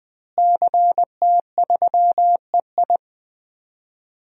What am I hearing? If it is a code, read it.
Morse code, heard as a keyed tone, about 20 wpm, 700 Hz. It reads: CT3EI
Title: DXCC-100 Call Signs - 5 Characters - Repeated 2x - 20wpm